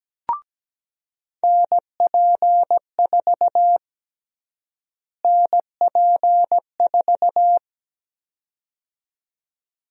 Morse code, heard as a keyed tone, about 17 wpm, 700 Hz. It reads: NP4 NP4
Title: USA Call Sign Prefixes - Repeated 2x - 17wpm